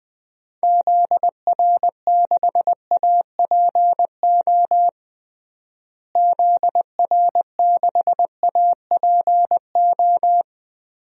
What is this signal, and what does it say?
Morse code, heard as a keyed tone, about 20 wpm, 700 Hz. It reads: ZR6APO ZR6APO
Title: DXCC-100 Call Signs - 6 Characters - Repeated 3x - 20wpm